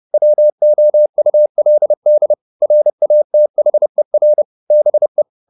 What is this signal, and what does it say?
Morse code, heard as a keyed tone, about 30 wpm, 600 Hz. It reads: WOULD RATHER BE
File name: the old man and the sea / the old man and the sea 1 30wpm